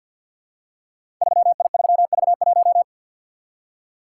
Morse code, heard as a keyed tone, about 51 wpm, 700 Hz. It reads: 3I3V1
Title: Code Groups - 5 Letter:Numbers - 50wpm